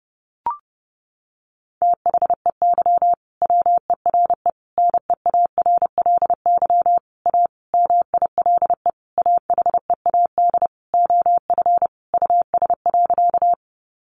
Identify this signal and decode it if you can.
Morse code, heard as a keyed tone, about 30 wpm, 700 Hz. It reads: THEY WERE NEARLY A MILE AHEAD OF US.
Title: Sentences from Top 1300 Words 30wpm